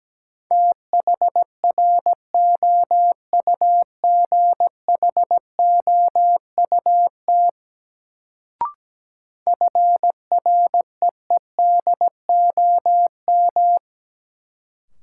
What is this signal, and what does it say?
Morse code, heard as a keyed tone, about 17 wpm, 700 Hz. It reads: THROUGHOUT FREEDOM
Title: Sentences from Top 800 Words - New Word Drill - 17wpm